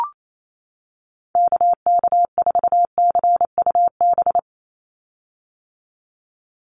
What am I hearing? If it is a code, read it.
Morse code, heard as a keyed tone, about 28 wpm, 700 Hz. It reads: KK4CUB
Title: USA Call Signs 28wpm